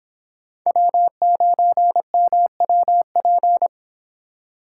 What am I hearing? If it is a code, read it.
Morse code, heard as a keyed tone, about 26 wpm, 700 Hz. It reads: W9MWP